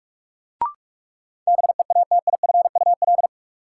Morse code, heard as a keyed tone, about 46 wpm, 700 Hz. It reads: BEATIFUL
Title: Sentences from Top 1300 Words - New Word Drill Alt - 45wpm